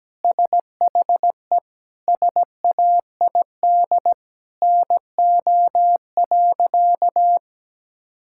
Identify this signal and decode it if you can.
Morse code, heard as a keyed tone, about 17 wpm, 700 Hz. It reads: SHE SAID NO.
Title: Sentences from Top 1000 Words - Review Part 6 - 17wpm